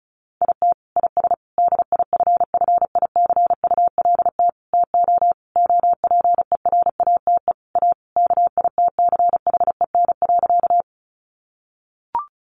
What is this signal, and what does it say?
Morse code, heard as a keyed tone, about 34 wpm, 700 Hz. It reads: IT IS DIFFICULT TO OPERATE A KITCHEN.
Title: Sentences from Top 1500 Words 35wpm